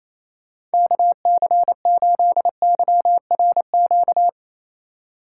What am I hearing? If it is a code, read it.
Morse code, heard as a keyed tone, about 34 wpm, 700 Hz. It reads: KC8YRQ
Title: Speed Racing - USA 2x3 Call Signs 28wpm